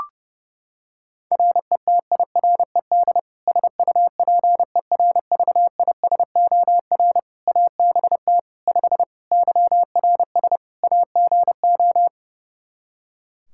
Morse code, heard as a keyed tone, about 30 wpm, 700 Hz. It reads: RETIRED SUPERVISOR ABT 5 YRS AGO
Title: Practice QSO - Exchange 3 - More Info - 30wpm